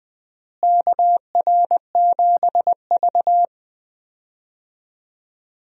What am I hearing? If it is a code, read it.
Morse code, heard as a keyed tone, about 21 wpm, 700 Hz. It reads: KR7V
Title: USA Call Signs - 4 Characters - Repeated 2x - 20wpm